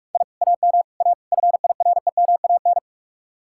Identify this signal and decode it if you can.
Morse code, heard as a keyed tone, about 45 wpm, 700 Hz. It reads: I AM A FIREMAN